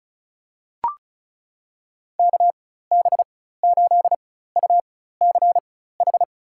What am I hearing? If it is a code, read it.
Morse code, heard as a keyed tone, about 36 wpm, spaced as for 16 wpm, 700 Hz. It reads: KB8UCH
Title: USA Call Signs - 4x Character Spacing - 35wpm